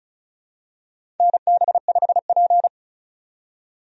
Morse code, heard as a keyed tone, about 36 wpm, 690 Hz. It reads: NB5P
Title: USA Call Signs - 4 Characters - 35wpm